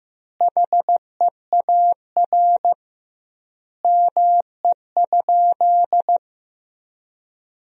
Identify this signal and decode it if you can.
Morse code, heard as a keyed tone, about 15 wpm, 700 Hz. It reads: HEAR ME?
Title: 2x Word-Spacing - Sentences from Top 300 Words - 15wpm